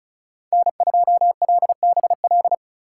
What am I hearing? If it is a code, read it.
Morse code, heard as a keyed tone, about 34 wpm, 700 Hz. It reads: N2LBL
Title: USA Call Signs - 5 Characters - 4x Character Spacing - 35wpm